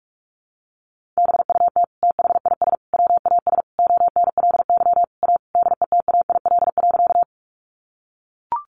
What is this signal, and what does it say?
Morse code, heard as a keyed tone, about 44 wpm, 700 Hz. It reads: BUT THIS WAS ONLY A DETAIL.
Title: Sentences from Top 1500 Words 45wpm